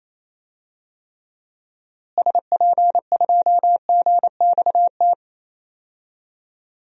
Morse code, heard as a keyed tone, about 28 wpm, 700 Hz. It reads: SP2GXT
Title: DXCC-100 Call Signs - 6 Characters - Repeated 2x - 28wpm